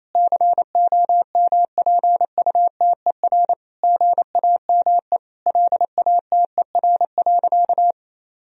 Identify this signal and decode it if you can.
Morse code, heard as a keyed tone, about 28 wpm, 700 Hz. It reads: COMPUTER GAME LATER.
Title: Sentences from Top 1000 Words - Review Part 3 - 28wpm